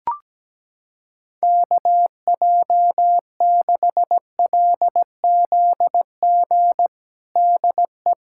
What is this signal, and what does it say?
Morse code, heard as a keyed tone, about 17 wpm, 700 Hz. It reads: KJ6LZG DE
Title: USA Callsign DE Callsign - 6 Characters - 17wpm